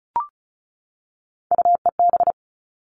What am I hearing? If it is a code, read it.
Morse code, heard as a keyed tone, about 34 wpm, 700 Hz. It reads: UEB
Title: All Cut Numbers - 3-Digit 35wpm